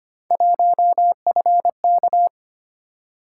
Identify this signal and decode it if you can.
Morse code, heard as a keyed tone, about 25 wpm, 700 Hz. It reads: 1FK
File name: Three Letter-Number Combinations - 25wpm